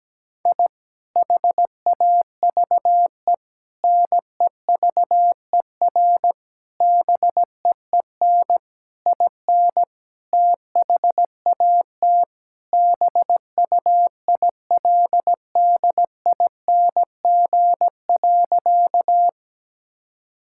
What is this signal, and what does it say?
Morse code, heard as a keyed tone, about 17 wpm, 700 Hz. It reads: I HAVE NEVER BEEN IN THAT BUILDING.
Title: Sentences from Top 1000 Words - Review Part 5 - 17wpm